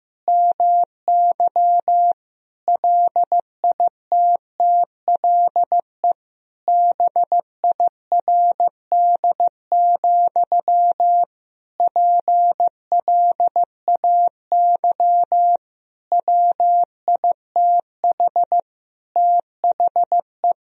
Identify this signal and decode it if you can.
Morse code, heard as a keyed tone, about 16 wpm, 700 Hz. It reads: MY LITTLE BIRD, PLAY WITH THE
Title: Sentences from Top 1000 Words - Review Part 2 - 15wpm